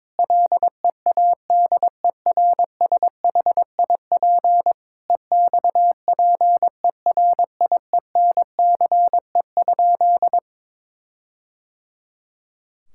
Morse code, heard as a keyed tone, about 22 wpm, 700 Hz. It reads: LEADERSHIP EXPERIENCE?
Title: Sentences from Top 1600 Words 22wpm